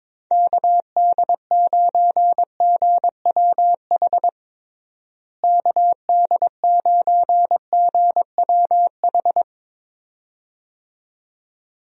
Morse code, heard as a keyed tone, about 22 wpm, 700 Hz. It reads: KD9GWH KD9GWH
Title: USA Call Signs - 6 Characters - Repeated 2x - 22wpm